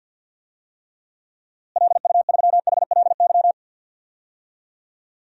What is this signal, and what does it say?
Morse code, heard as a keyed tone, about 52 wpm, 700 Hz. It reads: LU3HLY
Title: DXCC-100 Call Signs - 6 Characters - Repeated 3x - 50wpm